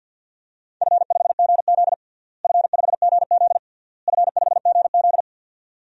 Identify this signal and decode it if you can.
Morse code, heard as a keyed tone, about 48 wpm, 700 Hz. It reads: F5GZ F5GZ F5GZ
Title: DXCC-100 Call Signs - 4 Characters - Repeated 3x - 50wpm